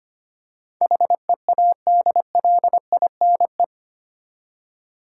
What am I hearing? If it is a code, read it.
Morse code, heard as a keyed tone, about 25 wpm, 700 Hz. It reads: HEADLINE
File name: Eight Letter Words - 25wpm